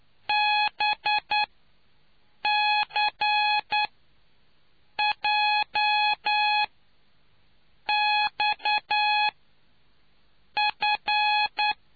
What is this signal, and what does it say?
Morse code, heard as a keyed tone, about 9 wpm, spaced as for 5 wpm, 840 Hz. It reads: BCJXF